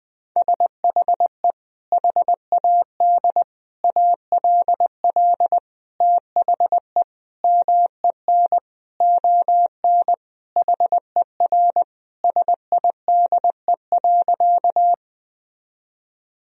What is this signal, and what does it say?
Morse code, heard as a keyed tone, about 20 wpm, 700 Hz. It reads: SHE HAD ALL THE MEN ON HER SIDE.
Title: Sentences from Top 1000 Words - Review Part 1 - 20wpm